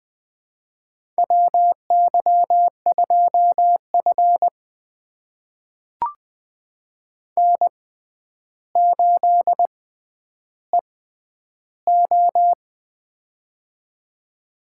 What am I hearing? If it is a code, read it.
Morse code, heard as a keyed tone, about 20 wpm, spaced as for 10 wpm, 700 Hz. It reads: WY2F N8EO
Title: USA Call Signs - 4 Characters - 6x Character Spacing - 20wpm